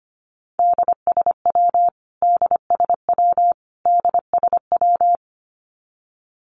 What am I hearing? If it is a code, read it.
Morse code, heard as a keyed tone, about 25 wpm, 700 Hz. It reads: DSW DSW DSW
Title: Abbreviations Repeated 3x - 25wpm